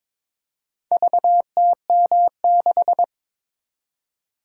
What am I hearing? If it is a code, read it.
Morse code, heard as a keyed tone, about 22 wpm, 700 Hz. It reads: VTM6
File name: practice / 22wpm / Four Letter-Number Combinations - 22wpm